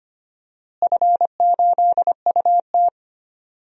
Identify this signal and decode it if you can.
Morse code, heard as a keyed tone, about 25 wpm, 700 Hz. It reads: F8UT
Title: DXCC-100 Call Signs - 4 Characters - Repeated 3x - 25wpm